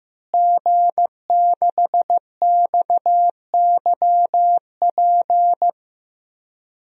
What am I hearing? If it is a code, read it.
Morse code, heard as a keyed tone, about 15 wpm, 700 Hz. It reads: G6XYP
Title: DXCC-100 Call Signs - 5 Characters - Repeated 3x - 15wpm